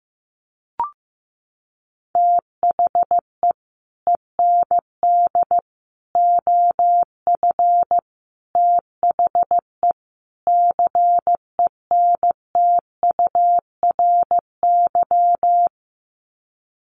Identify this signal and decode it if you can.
Morse code, heard as a keyed tone, about 15 wpm, 700 Hz. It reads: THE END OF THE CENTURY
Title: Sets of 5 Words Encore2 15wpm